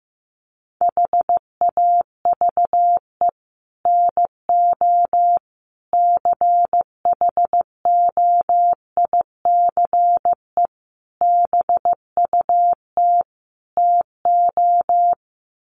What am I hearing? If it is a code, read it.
Morse code, heard as a keyed tone, about 15 wpm, 700 Hz. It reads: HAVE NO CHOICE BUT TO
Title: Sets of 5 Words Encore 15wpm